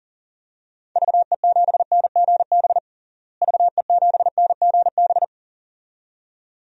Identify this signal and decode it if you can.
Morse code, heard as a keyed tone, about 40 wpm, 700 Hz. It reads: VE7NGB VE7NGB
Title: DXCC-100 Call Signs - 6 Characters - Repeated 2x - 40wpm